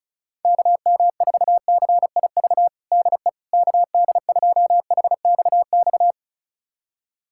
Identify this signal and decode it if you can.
Morse code, heard as a keyed tone, about 36 wpm, 700 Hz. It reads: KM4CIV DE KD2HXX
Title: USA Callsign DE Callsign 35wpm